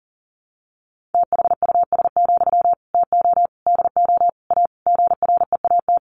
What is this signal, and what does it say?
Morse code, heard as a keyed tone, about 40 wpm, 700 Hz. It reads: THUS, TO DO A GREAT
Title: Sentences from Top 900 Words 40wpm